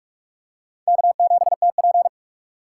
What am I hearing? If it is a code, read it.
Morse code, heard as a keyed tone, about 44 wpm, 700 Hz. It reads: K7TP
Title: USA Call Signs - 4 Characters - 6x Character Spacing - 45wpm